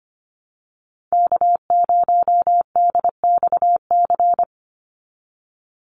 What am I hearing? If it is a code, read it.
Morse code, heard as a keyed tone, about 25 wpm, 700 Hz. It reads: K0DXC
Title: Call Signs Repeated 3x - 25wpm